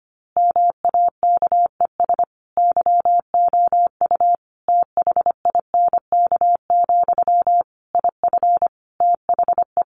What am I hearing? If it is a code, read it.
Morse code, heard as a keyed tone, about 25 wpm, 700 Hz. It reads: MAKES YOU THINK, IF THE
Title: A Sound Of Thunder 25wpm